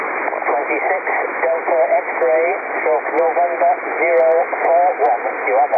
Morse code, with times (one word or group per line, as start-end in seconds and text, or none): none